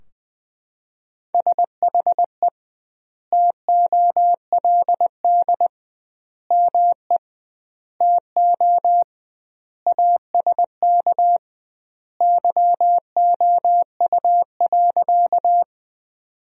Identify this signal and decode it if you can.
Morse code, heard as a keyed tone, about 21 wpm, 700 Hz. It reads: SHE TOLD ME TO ASK YOU.
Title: 2x Word-Spacing - Sentences from Top 400 Words - 20wpm